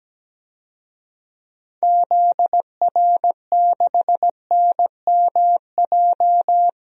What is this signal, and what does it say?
Morse code, heard as a keyed tone, about 17 wpm, 700 Hz. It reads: ZR6NMJ
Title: DXCC-100 Call Signs - 6 Characters - 17wpm